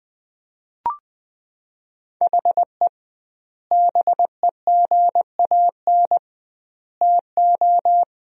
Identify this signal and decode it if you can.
Morse code, heard as a keyed tone, about 20 wpm, 700 Hz. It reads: HE BEGAN TO
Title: 2x Word-Spacing - Sentences from Top 500 Words - 20wpm